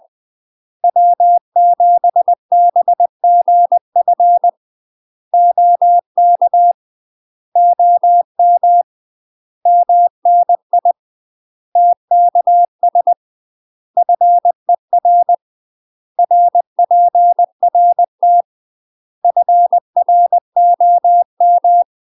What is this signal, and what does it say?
Morse code, heard as a keyed tone, about 20 wpm, 700 Hz. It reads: W7BGF OK OM MNI TKS FER RPRT FROM
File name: UK MASTER QSOs TRAINING / 20WPM / UK MASTER QSOs 0060